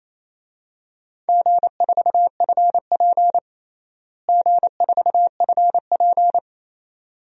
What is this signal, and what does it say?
Morse code, heard as a keyed tone, about 28 wpm, 700 Hz. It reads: G4FP G4FP
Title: DXCC-100 Call Signs - 4 Characters - Repeated 2x - 28wpm